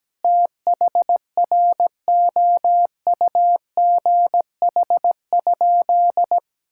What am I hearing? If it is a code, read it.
Morse code, heard as a keyed tone, about 17 wpm, 700 Hz. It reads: THROUGH?
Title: Sentences from Top 1000 Words - Review Part 6 - 17wpm